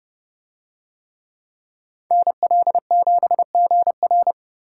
Morse code, heard as a keyed tone, about 29 wpm, 700 Hz. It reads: NL7GR